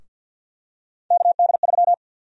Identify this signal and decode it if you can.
Morse code, heard as a keyed tone, about 52 wpm, 700 Hz. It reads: KD3
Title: USA Call Sign Prefixes - 50wpm